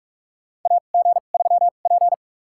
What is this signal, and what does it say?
Morse code, heard as a keyed tone, about 47 wpm, spaced as for 33 wpm, 700 Hz. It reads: AG3P
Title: USA Call Signs - 2x Character Spacing - 45wpm